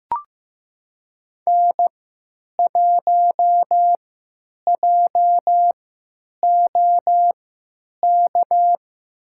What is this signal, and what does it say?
Morse code, heard as a keyed tone, about 15 wpm, spaced as for 9 wpm, 700 Hz. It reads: N1JOK
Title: USA Call Signs - 5 Characters - 3x Character Spacing - 15wpm